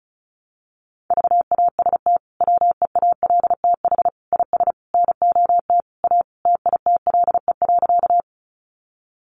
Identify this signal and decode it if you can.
Morse code, heard as a keyed tone, about 34 wpm, 700 Hz. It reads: VAST WEALTH IS NOT A TITLE.
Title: Sentences from Top 1300 Words 35wpm